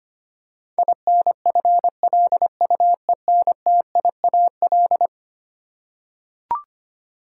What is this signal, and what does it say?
Morse code, heard as a keyed tone, about 25 wpm, 700 Hz. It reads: INFLUENTIAL
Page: morse-code-ninja-practice-25wpm